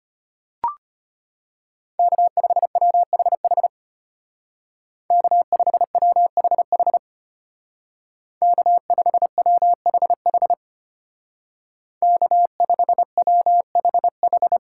Morse code, heard as a keyed tone, about 32 wpm, 700 Hz. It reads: K5WHH K5WHH K5WHH K5WHH